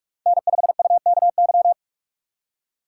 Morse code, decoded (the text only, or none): N5UKY